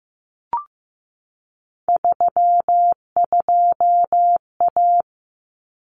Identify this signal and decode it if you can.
Morse code, heard as a keyed tone, about 15 wpm, 700 Hz. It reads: 32A
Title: Common Cut Numbers - 3-Digit 15wpm